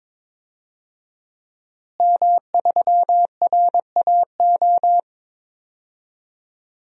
Morse code, heard as a keyed tone, about 22 wpm, 700 Hz. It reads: M3RAO
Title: DXCC-100 Call Signs - 5 Characters - Repeated 3x - 22wpm